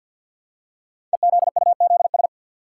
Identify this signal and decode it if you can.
Morse code, heard as a keyed tone, about 46 wpm, 700 Hz. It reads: EZUZS
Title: Code Groups - 5 Letters - 50wpm